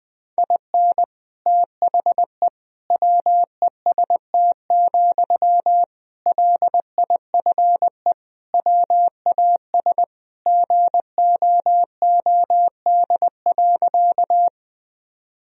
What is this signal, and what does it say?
Morse code, heard as a keyed tone, about 21 wpm, 700 Hz. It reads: IN THE WEST, LIFE WAS GOOD.
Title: Sentences from Top 1000 Words - Review Part 2 - 20wpm